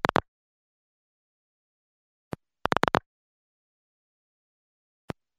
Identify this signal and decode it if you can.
Pulse dialling.
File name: Pulse dial sounds in the handset